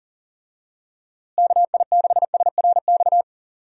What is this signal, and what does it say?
Morse code, heard as a keyed tone, about 37 wpm, 700 Hz. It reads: KI6SRX
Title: USA Call Signs - 6 Characters - Repeated 2x - 40wpm